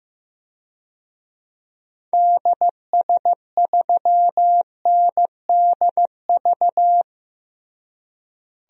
Morse code, heard as a keyed tone, about 16 wpm, 700 Hz. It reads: DS3NDV
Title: DXCC-100 Call Signs - 6 Characters - Repeated 2x - 15wpm